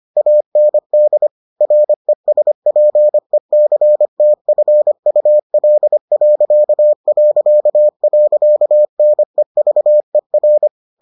Morse code, heard as a keyed tone, about 25 wpm, 600 Hz. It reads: AND RESPECTFUL...NEVER